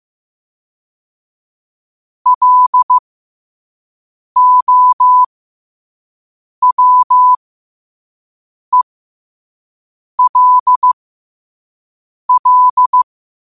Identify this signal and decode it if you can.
Morse code, heard as a keyed tone, about 15 wpm, spaced as for 5 wpm, 1000 Hz. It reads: LOWELL
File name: qotdfiles / 5wpm